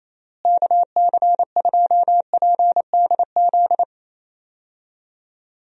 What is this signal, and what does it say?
Morse code, heard as a keyed tone, about 28 wpm, 700 Hz. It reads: KC2PDZ
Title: USA Call Signs - 6 Characters - Repeated 3x - 28wpm